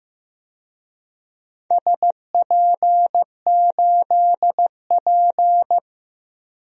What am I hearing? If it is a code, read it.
Morse code, heard as a keyed tone, about 15 wpm, 700 Hz. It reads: SP8P
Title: DXCC-100 Call Signs - 4 Characters - Repeated 2x - 15wpm